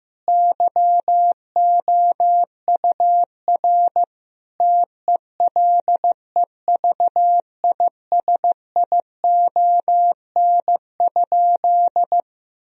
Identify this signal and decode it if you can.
Morse code, heard as a keyed tone, about 16 wpm, 700 Hz. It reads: YOUR TELEVISION?
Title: Sentences from Top 1000 Words - Review Part 4 - 15wpm